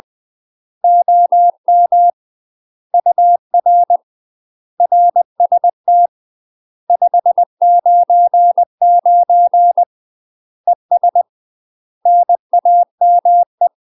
Morse code, heard as a keyed tone, about 20 wpm, 700 Hz. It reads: OM UR RST 599 ES NAME
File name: UK MASTER QSOs TRAINING / 20WPM / UK MASTER QSOs 0002